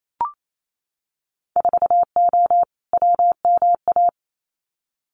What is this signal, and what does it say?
Morse code, heard as a keyed tone, about 28 wpm, 700 Hz. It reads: 4O WMA
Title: Winter Field Day Exchanges Encore 28wpm